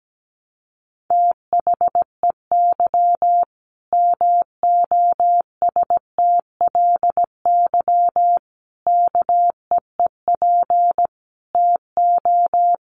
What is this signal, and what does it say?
Morse code, heard as a keyed tone, about 17 wpm, 700 Hz. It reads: THEY MOSTLY KEEP TO
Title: Sentences from Top 1300 Words 17wpm